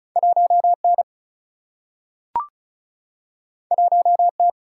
Morse code, heard as a keyed tone, about 36 wpm, 700 Hz. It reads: 1N 1T
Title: Single-Letter Single-Number 35wpm